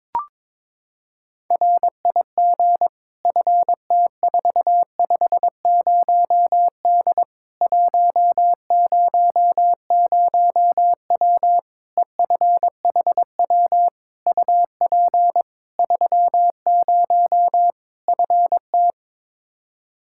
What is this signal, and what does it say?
Morse code, heard as a keyed tone, about 22 wpm, 700 Hz. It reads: RIG FT450D 100W EFHW UP 30 FT
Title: K1USN SST QSO Practice 22wpm